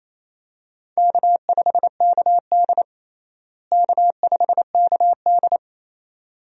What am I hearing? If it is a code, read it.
Morse code, heard as a keyed tone, about 28 wpm, 700 Hz. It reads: K5KD K5KD
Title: USA Call Signs - 4 Characters - Repeated 2x - 28wpm